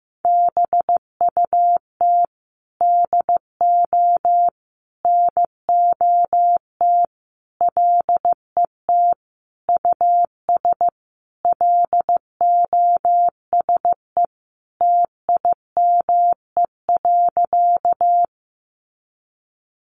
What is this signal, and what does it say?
Morse code, heard as a keyed tone, about 15 wpm, 700 Hz. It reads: BUT DO NOT LET US LOSE TIME.
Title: Sentences from Top 1100 Words 15wpm